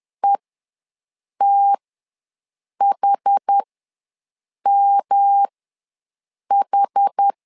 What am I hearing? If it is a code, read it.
Morse code, heard as a keyed tone, about 10 wpm, spaced as for 6 wpm, 800 Hz. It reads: ETHMH
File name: LEZIONI / 3 MWRO / 3 M ws 10wpm 2121